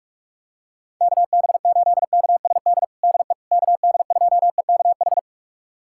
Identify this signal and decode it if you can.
Morse code, heard as a keyed tone, about 46 wpm, 700 Hz. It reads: KB8KSD DE KD2EKH
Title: USA Callsign DE Callsign 45wpm